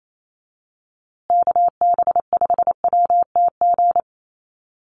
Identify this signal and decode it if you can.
Morse code, heard as a keyed tone, about 28 wpm, 700 Hz. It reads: KB5WTG
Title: USA Call Signs Repeated 3x 28wpm